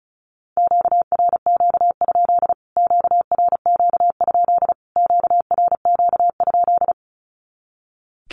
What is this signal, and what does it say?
Morse code, heard as a keyed tone, about 34 wpm, 700 Hz. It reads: QRQ? QRQ? QRQ?
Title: Q-Codes Repeated 3x 35wpm